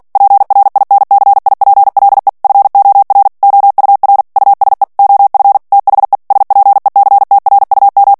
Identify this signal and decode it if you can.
Morse code, heard as a keyed tone, about 48 wpm, 800 Hz. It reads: PRINCIPLE FOR OUR USE OF THE SPECTRUM
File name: ANGLAIS texte clair 3 vitesse 45 mots minute